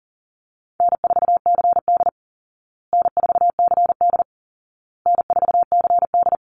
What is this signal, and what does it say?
Morse code, heard as a keyed tone, about 40 wpm, 700 Hz. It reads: N4CD N4CD N4CD
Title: Call Signs Repeated 2x - 40wpm